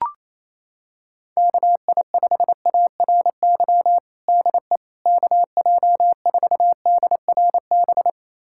Morse code, heard as a keyed tone, about 28 wpm, 700 Hz. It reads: KI5ARY DE KJ4DRB
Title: USA Callsign DE Callsign 28wpm